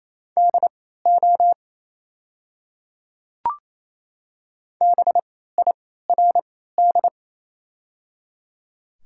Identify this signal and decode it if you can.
Morse code, heard as a keyed tone, about 28 wpm, spaced as for 16 wpm, 690 Hz. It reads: DO BIRD
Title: 3x Character Spacing - Top 500 Words - 28wpm